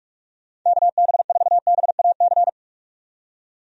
Morse code, heard as a keyed tone, about 41 wpm, 700 Hz. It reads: KB4BAC
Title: USA Call Signs - 6 Characters - 45wpm